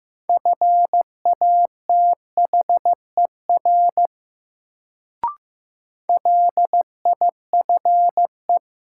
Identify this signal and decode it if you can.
Morse code, heard as a keyed tone, about 15 wpm, 700 Hz. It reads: FATHER LIFE
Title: Sentences from Top 300 Words - New Word Drill - 15wpm